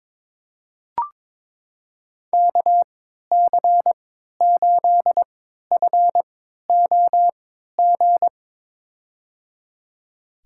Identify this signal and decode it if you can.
Morse code, heard as a keyed tone, about 23 wpm, spaced as for 13 wpm, 700 Hz. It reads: KC8FOG
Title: USA Call Signs - 6 Characters - 3x Character Spacing - 22wpm